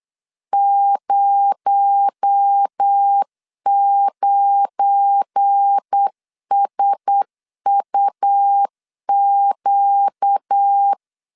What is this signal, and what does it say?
Morse code, heard as a keyed tone, about 8 wpm, 800 Hz. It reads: 09SUQ
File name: CW / DET / 1 9 1500C S C 8wpm 5653